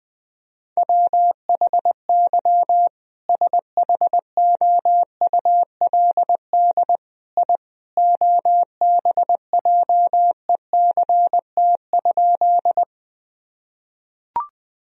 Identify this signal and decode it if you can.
Morse code, heard as a keyed tone, about 20 wpm, 700 Hz. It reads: WHY SHOULD I OBJECT?
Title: Sentences from Top 1000 Words - Review Part 3 - 20wpm